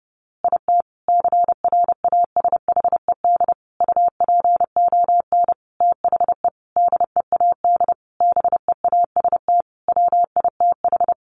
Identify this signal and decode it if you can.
Morse code, heard as a keyed tone, about 30 wpm, 700 Hz. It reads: IT CRASHED UPON THE DEAD BEAST WITH
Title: A Sound Of Thunder 30wpm